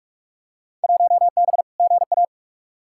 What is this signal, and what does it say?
Morse code, heard as a keyed tone, about 46 wpm, 700 Hz. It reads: 1B GA
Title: Field Day Exchanges Encore - 45wpm